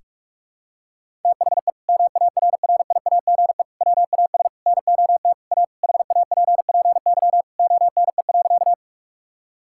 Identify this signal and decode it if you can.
Morse code, heard as a keyed tone, about 47 wpm, 700 Hz. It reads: THE MARRIAGE WAS NOT A HAPPY ONE.
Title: Sentences from Top 1000 Words - Review Part 5 - 45wpm